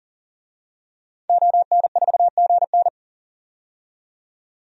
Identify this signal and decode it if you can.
Morse code, heard as a keyed tone, about 40 wpm, 700 Hz. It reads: ON4GN
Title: DXCC-100 Call Signs - 5 Characters - 40wpm